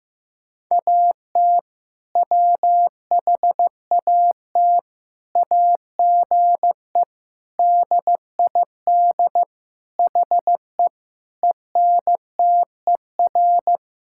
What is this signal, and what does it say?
Morse code, heard as a keyed tone, about 15 wpm, 690 Hz. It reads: AT WHAT AGE DID HE ENTER
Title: Sentences from Top 1000 Words - Review Part 5 - 15wpm